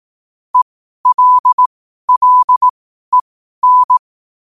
Morse code, heard as a keyed tone, about 18 wpm, spaced as for 12 wpm, 1000 Hz. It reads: ELLEN
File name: qotdfiles / 12wpm